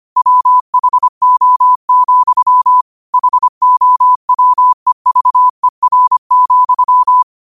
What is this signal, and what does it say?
Morse code, heard as a keyed tone, about 25 wpm, 1000 Hz. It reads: WHO, HOWEVER,
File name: qotdfiles / 25wpm